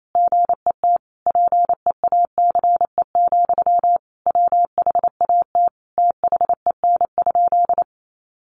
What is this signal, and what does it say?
Morse code, heard as a keyed tone, about 28 wpm, 700 Hz. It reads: GET PEACE, WHAT THEN?
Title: Sentences from Top 1500 Words 28wpm